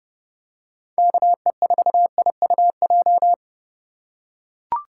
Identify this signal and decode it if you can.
Morse code, heard as a keyed tone, about 30 wpm, 700 Hz. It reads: KE4IUJ